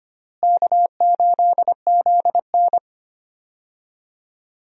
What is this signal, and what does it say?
Morse code, heard as a keyed tone, about 25 wpm, 700 Hz. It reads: K8ZN
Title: USA Call Signs - 4 Characters - Repeated 2x - 25wpm